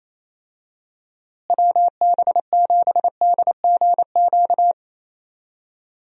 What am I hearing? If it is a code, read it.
Morse code, heard as a keyed tone, about 28 wpm, 700 Hz. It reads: WB7DGQ
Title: USA Call Signs - 6 Characters - Repeated 2x - 28wpm